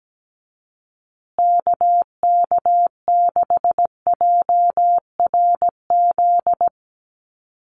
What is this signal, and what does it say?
Morse code, heard as a keyed tone, about 17 wpm, 700 Hz. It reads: KK6JRZ
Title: USA Call Signs Repeated 2x 17wpm